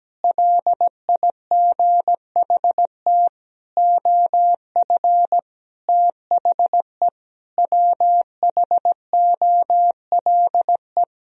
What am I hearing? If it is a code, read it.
Morse code, heard as a keyed tone, about 17 wpm, 700 Hz. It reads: LIGHT OF THE WHOLE
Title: Sentences from Top 1000 Words - Review Part 1 - 17wpm